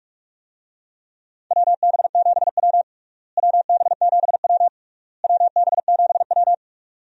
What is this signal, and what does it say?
Morse code, heard as a keyed tone, about 41 wpm, 700 Hz. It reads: WB7W WB7W WB7W
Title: USA Call Signs - 4 Characters - Repeated 3x - 45wpm